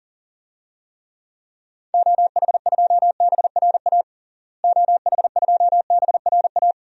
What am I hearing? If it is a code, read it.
Morse code, heard as a keyed tone, about 40 wpm, 700 Hz. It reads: OH2BRA OH2BRA
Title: DXCC-100 Call Signs - 6 Characters - Repeated 2x - 40wpm